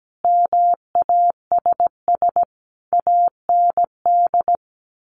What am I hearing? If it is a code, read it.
Morse code, heard as a keyed tone, about 17 wpm, 700 Hz. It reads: MASS AND
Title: A Sound Of Thunder 17wpm